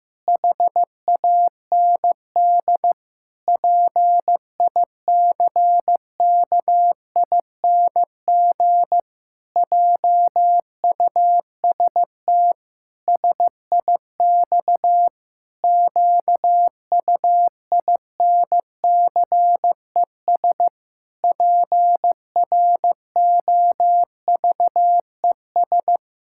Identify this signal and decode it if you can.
Morse code, heard as a keyed tone, about 15 wpm, 700 Hz. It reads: HAND PICKING JUST SIX QUINCES PROVES
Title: Pangram Challenge v2 - 15wpm